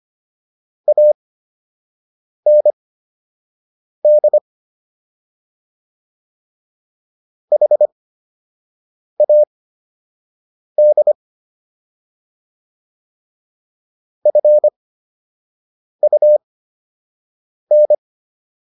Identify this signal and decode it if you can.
Morse code, heard as a keyed tone, about 25 wpm, spaced as for 6 wpm, 600 Hz. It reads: AND HAD FUN